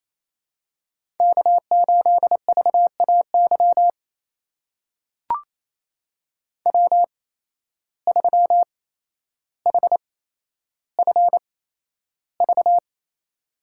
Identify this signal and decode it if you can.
Morse code, heard as a keyed tone, about 28 wpm, spaced as for 12 wpm, 700 Hz. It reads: K8VAY W3HFV
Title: USA Call Signs - 5 Characters - 8x Character Spacing - 28wpm